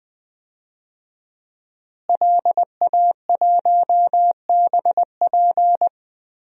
Morse code, heard as a keyed tone, about 20 wpm, 700 Hz. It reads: LA1BP
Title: DXCC-100 Call Signs - 5 Characters - Repeated 3x - 20wpm